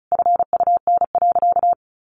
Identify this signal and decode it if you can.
Morse code, heard as a keyed tone, about 34 wpm, 700 Hz. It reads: FUN.
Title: Easy Phrases 35wpm